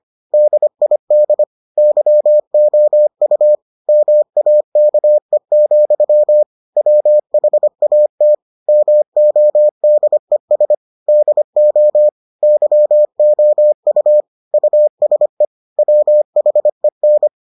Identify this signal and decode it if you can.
Morse code, heard as a keyed tone, about 25 wpm, 600 Hz. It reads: DID YOU MAKE, WHAT MODES DO YOU USE WHEN